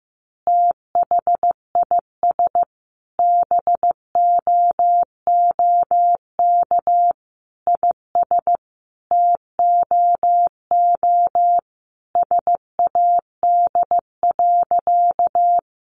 Morse code, heard as a keyed tone, about 15 wpm, 700 Hz. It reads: THIS BOOK IS TOO SAD.
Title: Additional Easy Phrases 15wpm